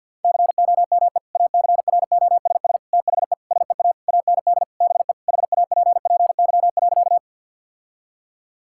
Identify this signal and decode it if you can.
Morse code, heard as a keyed tone, about 50 wpm, 700 Hz. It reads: COME ACROSS THE SEA AND BE HAPPY.
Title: Sentences from Top 1000 Words - Review Part 5 - 50wpm